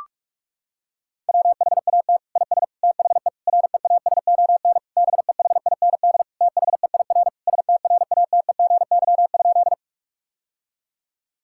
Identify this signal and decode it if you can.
Morse code, heard as a keyed tone, about 41 wpm, 690 Hz. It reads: WHAT IS THE REASON BEHIND THEIR STRATEGY?
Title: Sentences from Top 1000 Words - Review Part 5 - 45wpm